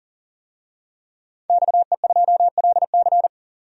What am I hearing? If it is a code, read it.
Morse code, heard as a keyed tone, about 40 wpm, 700 Hz. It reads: XE2LC
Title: DXCC-100 Call Signs - 5 Characters - 40wpm